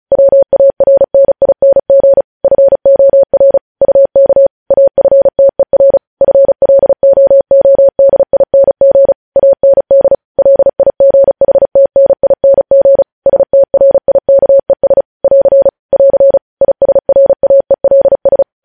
Morse code, heard as a keyed tone, about 36 wpm, 550 Hz. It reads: WARNING FOR UK AFTER FLOODING AND LIGHTNING STRIKES <AR> <AR> ISRAELS
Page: news headlines in morse code at 35 wpm